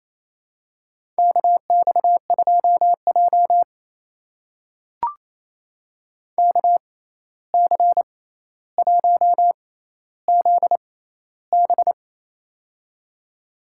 Morse code, heard as a keyed tone, about 28 wpm, spaced as for 13 wpm, 700 Hz. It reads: KX2J KC1ZB
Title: USA Call Signs - 6x Character Spacing - 28wpm